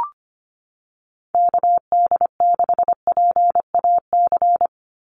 Morse code, heard as a keyed tone, about 25 wpm, 700 Hz. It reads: KD6PAC
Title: USA Call Signs 25wpm